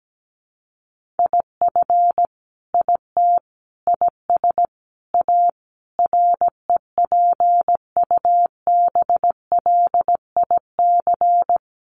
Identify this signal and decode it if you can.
Morse code, heard as a keyed tone, about 17 wpm, 700 Hz. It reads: IF IT IS A REPUBLIC
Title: Sentences from Top 1500 Words 17wpm